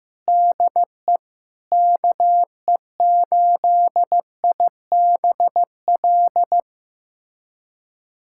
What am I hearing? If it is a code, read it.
Morse code, heard as a keyed tone, about 15 wpm, 700 Hz. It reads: DE KE8IBL
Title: USA Callsign DE Callsign - 6 Characters - 15wpm